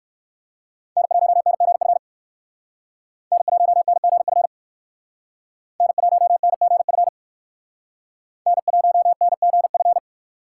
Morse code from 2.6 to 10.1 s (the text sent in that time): N1NGF N1NGF N1NGF